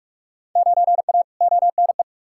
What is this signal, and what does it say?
Morse code, heard as a keyed tone, about 46 wpm, 700 Hz. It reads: 9A ONE
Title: Field Day Exchanges Encore - 45wpm